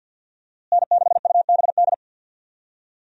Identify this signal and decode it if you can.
Morse code, heard as a keyed tone, about 48 wpm, 700 Hz. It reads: N6UBD
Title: USA Call Signs - 5 Characters - Repeated 3x - 50wpm